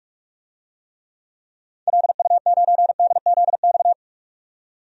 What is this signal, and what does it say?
Morse code, heard as a keyed tone, about 46 wpm, 700 Hz. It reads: LU9DZX
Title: DXCC-100 Call Signs - 6 Characters - 45wpm